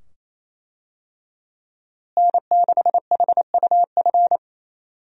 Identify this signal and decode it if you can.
Morse code, heard as a keyed tone, about 28 wpm, 700 Hz. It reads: N6HUF
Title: USA Call Signs - 5 Characters - Repeated 2x - 28wpm